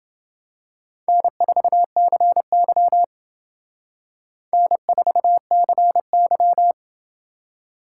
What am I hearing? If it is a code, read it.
Morse code, heard as a keyed tone, about 28 wpm, 700 Hz. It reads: N4CY N4CY